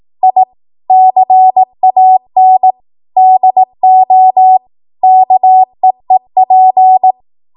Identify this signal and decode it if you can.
Morse code, heard as a keyed tone, about 18 wpm, 750 Hz. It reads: I CAN DO KEEP
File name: Archive / 18 WPM / 18WPM